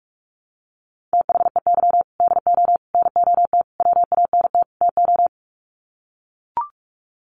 Morse code, heard as a keyed tone, about 44 wpm, 700 Hz. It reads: THEY DO NOT WANT TO
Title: Sets of 5 Words 45wpm